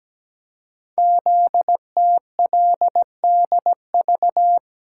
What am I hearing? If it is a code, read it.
Morse code, heard as a keyed tone, about 17 wpm, 700 Hz. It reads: ZTLDV
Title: Code Groups - 5 Letters - 17wpm